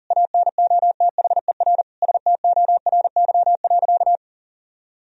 Morse code, heard as a keyed tone, about 40 wpm, 700 Hz. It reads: ANOTHER STORY.
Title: Sentences from Top 1000 Words - Review Part 1 - 40wpm